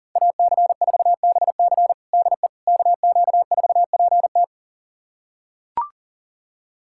Morse code, heard as a keyed tone, about 40 wpm, 700 Hz. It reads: AC4BC DE KQ4PT
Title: USA Callsign DE Callsign - 5 Characters - 40wpm